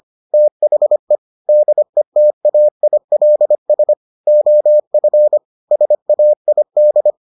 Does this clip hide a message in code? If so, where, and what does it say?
Yes: Morse code from 0.3 to 7.1 s, heard as a keyed tone, about 25 wpm, 600 Hz. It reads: THE DETAILS OF SAID